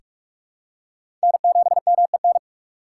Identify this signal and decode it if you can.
Morse code, heard as a keyed tone, about 43 wpm, 700 Hz. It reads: N7MEN
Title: USA Call Signs - 5 Characters - 3x Character Spacing - 45wpm